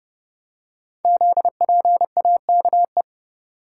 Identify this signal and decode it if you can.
Morse code, heard as a keyed tone, about 30 wpm, 700 Hz. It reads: ZPAKE
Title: Code Groups - 5 Letter:Numbers - 30wpm